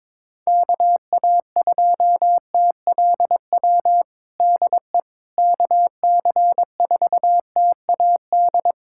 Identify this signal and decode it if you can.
Morse code, heard as a keyed tone, about 22 wpm, 700 Hz. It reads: KA2TLW DE KC4TAD
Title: USA Callsign DE Callsign - 6 Characters - 22wpm